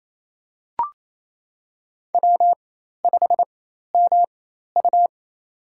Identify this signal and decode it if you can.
Morse code, heard as a keyed tone, about 28 wpm, spaced as for 13 wpm, 700 Hz. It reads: W5MU
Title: USA Call Signs - 4x Character Spacing - 28wpm